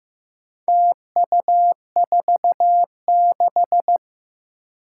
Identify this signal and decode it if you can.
Morse code, heard as a keyed tone, about 15 wpm, 700 Hz. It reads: TU46
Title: Four Challenging Letter-Number Combinations 15wpm